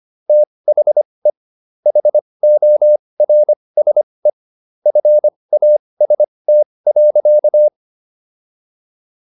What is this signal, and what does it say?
Morse code, heard as a keyed tone, about 25 wpm, spaced as for 20 wpm, 600 Hz. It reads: THE HORSE FAST.